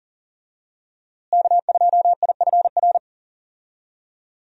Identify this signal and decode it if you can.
Morse code, heard as a keyed tone, about 40 wpm, 700 Hz. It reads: K2IFR
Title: USA Call Signs - 5 Characters - Repeated 3x - 40wpm